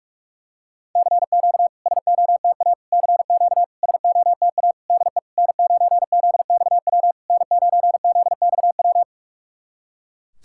Morse code, heard as a keyed tone, about 44 wpm, 690 Hz. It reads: CQ SOTA CQ SOTA DE N9ZXW N9ZXW
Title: SOTA QSO Practice 45wpm